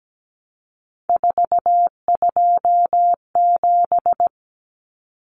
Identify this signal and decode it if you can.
Morse code, heard as a keyed tone, about 17 wpm, 700 Hz. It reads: 427
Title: Common Cut Numbers - 3-Digit 17wpm